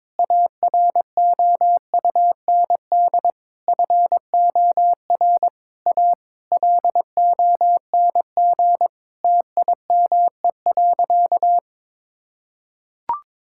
Morse code, heard as a keyed tone, about 21 wpm, 700 Hz. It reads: AROUND FOR A LONG TIME.
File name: Sentences with 7-Word Phrases - 22wpm